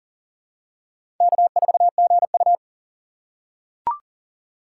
Morse code, heard as a keyed tone, about 40 wpm, 700 Hz. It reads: K4GU
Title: USA Call Signs - 4 Characters - 3x Character Spacing - 40wpm